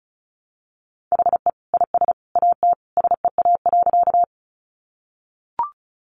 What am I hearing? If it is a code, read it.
Morse code, heard as a keyed tone, about 34 wpm, 700 Hz. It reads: HE IS AT SEA.
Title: Sentences from Top 300 Words 35wpm